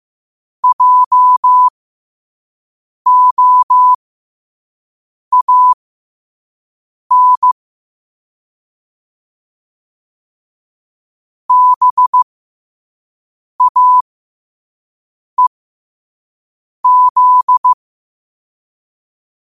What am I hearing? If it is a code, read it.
Morse code, heard as a keyed tone, about 15 wpm, spaced as for 5 wpm, 1000 Hz. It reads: JOAN BAEZ
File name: qotdfiles / 5wpm